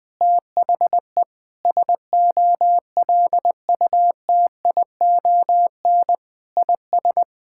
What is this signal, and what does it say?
Morse code, heard as a keyed tone, about 20 wpm, 700 Hz. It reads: THE SOLUTION IS
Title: Sentences with 9-Word Phrases - 20wpm